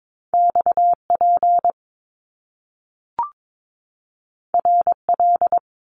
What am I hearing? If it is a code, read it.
Morse code, heard as a keyed tone, about 22 wpm, 700 Hz. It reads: XP RL
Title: Top Two Letter Combinations in Words 22wpm